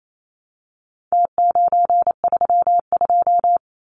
Morse code, heard as a keyed tone, about 28 wpm, 700 Hz. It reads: T932
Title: Numbers - Contest Style - 4-Digit 28wpm